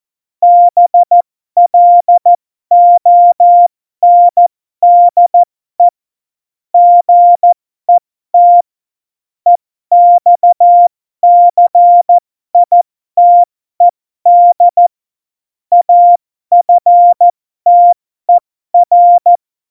Morse code, heard as a keyed tone, about 14 wpm, spaced as for 12 wpm, 700 Hz. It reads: BLONDE GET EXCITED AFTER
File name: Morsekurs / Witze